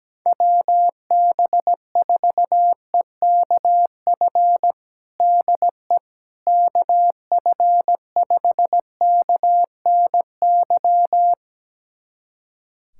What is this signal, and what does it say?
Morse code, heard as a keyed tone, about 17 wpm, 700 Hz. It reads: WB4EKF DE KF5KNY
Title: USA Callsign DE Callsign - 6 Characters - 17wpm